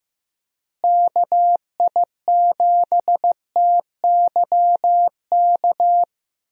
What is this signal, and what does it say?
Morse code, heard as a keyed tone, about 15 wpm, 700 Hz. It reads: KI7TYK
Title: USA Call Signs - 6 Characters - Repeated 2x - 15wpm